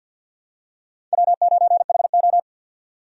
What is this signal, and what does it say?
Morse code, heard as a keyed tone, about 49 wpm, 700 Hz. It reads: W9HO
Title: USA Call Signs - 4 Characters - 6x Character Spacing - 50wpm